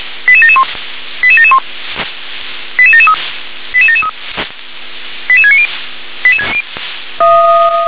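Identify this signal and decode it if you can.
Five-tone selective call (ZVEI), heard as quick runs of five tones, tones of about 70 ms. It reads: 88771 88771 88773 88773 88690
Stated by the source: ZVEI five-tone selective call 88713, received by SDR.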